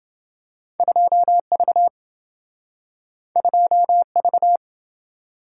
Morse code, heard as a keyed tone, about 27 wpm, 700 Hz. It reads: 2V 2V